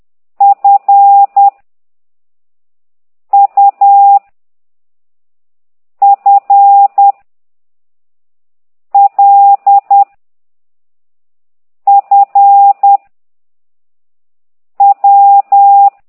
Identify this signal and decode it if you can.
Morse code, heard as a keyed tone, about 10 wpm, 800 Hz. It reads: F U F L F W
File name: ct2idw / exerc8